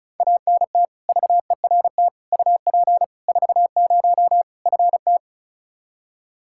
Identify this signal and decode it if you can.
Morse code, heard as a keyed tone, about 35 wpm, 700 Hz. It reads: ANT VERT UP 40 FT
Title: Practice QSO - Exchange 2 - Info - 35wpm